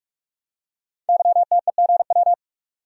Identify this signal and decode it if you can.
Morse code, heard as a keyed tone, about 44 wpm, 700 Hz. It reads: YTEGW
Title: Code Groups - 5 Letter:Numbers - 45wpm